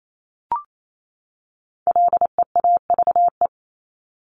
Morse code, heard as a keyed tone, about 28 wpm, 700 Hz. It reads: LEAVE
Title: Top 500 Words 28wpm